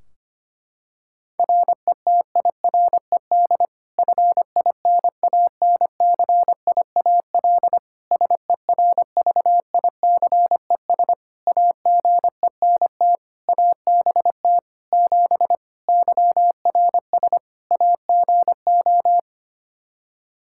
Morse code, heard as a keyed tone, about 25 wpm, 700 Hz. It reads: RETIRED FINANCIAL SERVICES AGENT ABT 7 YRS AGO
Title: K1USN SST QSO Practice 25wpm